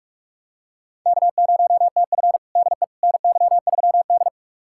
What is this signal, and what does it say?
Morse code, heard as a keyed tone, about 42 wpm, 700 Hz. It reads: K0TF DE NY3D
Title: USA Callsign DE Callsign - 4 Characters - 45wpm